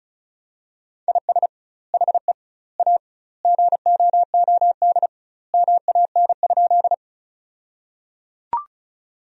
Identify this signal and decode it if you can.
Morse code, heard as a keyed tone, about 34 wpm, 700 Hz. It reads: IS HE A GOOD MAN?
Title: 2x Word-Spacing - Sentences from Top 200 Words - 35wpm